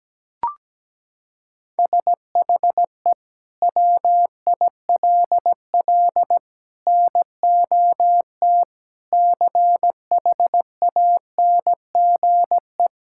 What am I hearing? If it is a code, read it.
Morse code, heard as a keyed tone, about 17 wpm, 700 Hz. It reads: SHE WILL NOT CHANGE
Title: Sentences from Top 1000 Words - Review Part 6 - 17wpm